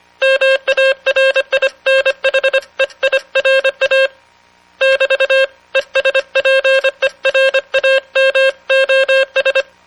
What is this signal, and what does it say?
Morse code, heard as a keyed tone, about 25 wpm, 540 Hz. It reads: MARINHEIRA <BT>ESPERAMOS